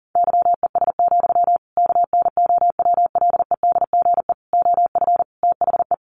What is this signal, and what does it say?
Morse code, heard as a keyed tone, about 40 wpm, 700 Hz. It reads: YES, KNOWLEDGE OF THE
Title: Sentences from Top 1500 Words 40wpm